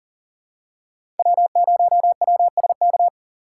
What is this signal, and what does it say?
Morse code, heard as a keyed tone, about 39 wpm, 700 Hz. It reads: W0WSK
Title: USA Call Signs - 5 Characters - 6x Character Spacing - 40wpm